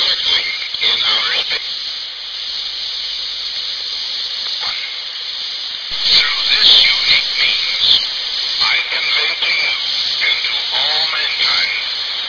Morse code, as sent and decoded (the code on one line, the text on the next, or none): none